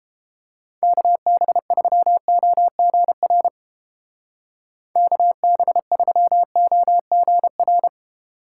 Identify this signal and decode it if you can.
Morse code, heard as a keyed tone, about 29 wpm, 700 Hz. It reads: KB3OGR KB3OGR